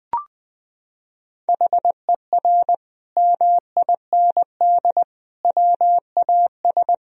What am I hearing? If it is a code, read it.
Morse code, heard as a keyed tone, about 20 wpm, 700 Hz. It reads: HER MIND WAS
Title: Sentences from Top 1000 Words - Review Part 2 - 20wpm